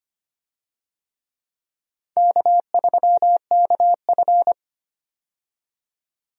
Morse code, heard as a keyed tone, about 25 wpm, 700 Hz. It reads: K3KF
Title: USA Call Signs - 4 Characters - 25wpm